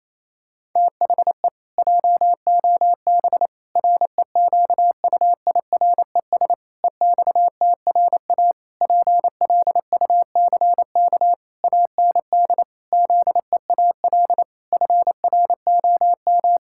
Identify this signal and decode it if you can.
Morse code, heard as a keyed tone, about 28 wpm, 700 Hz. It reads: THE JOB REQUIRES EXTRA PLUCK AND ZEAL FROM
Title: Pangram Challenge v2 - 28wpm